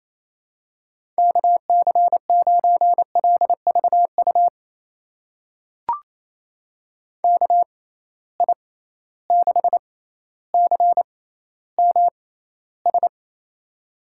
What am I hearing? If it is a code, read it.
Morse code, heard as a keyed tone, about 28 wpm, spaced as for 14 wpm, 700 Hz. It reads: KC9LVU KI6CMS
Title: USA Call Signs - 6 Characters - 6x Character Spacing - 28wpm